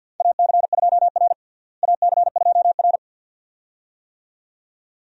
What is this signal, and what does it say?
Morse code, heard as a keyed tone, about 53 wpm, 700 Hz. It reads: AC2R AC2R
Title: USA Call Signs - 4 Characters - Repeated 2x - 50wpm